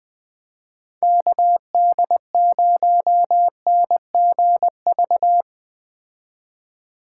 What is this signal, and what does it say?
Morse code, heard as a keyed tone, about 21 wpm, 700 Hz. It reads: KD0NGV